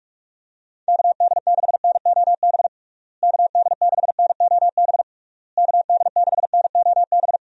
Morse code, heard as a keyed tone, about 42 wpm, 700 Hz. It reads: KD6NOB KD6NOB KD6NOB
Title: USA Call Signs - 6 Characters - Repeated 3x - 45wpm